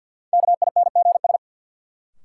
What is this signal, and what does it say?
Morse code, heard as a keyed tone, about 49 wpm, 700 Hz. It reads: KINGS
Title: Sentences from Top 1000 Words - Review Part 6 - New Word Drill - 50wpm